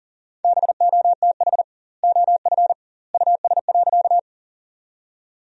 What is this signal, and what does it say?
Morse code, heard as a keyed tone, about 40 wpm, 700 Hz. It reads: BOTH OF US.
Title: 2x Word-Spacing - Sentences from Top 300 Words - 40wpm